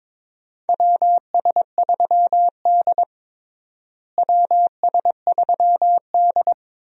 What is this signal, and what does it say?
Morse code, heard as a keyed tone, about 22 wpm, 700 Hz. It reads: WS3D WS3D
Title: USA Call Signs - 4 Characters - Repeated 2x - 22wpm